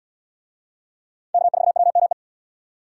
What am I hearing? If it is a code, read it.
Morse code, heard as a keyed tone, about 71 wpm, 700 Hz. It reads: K4WME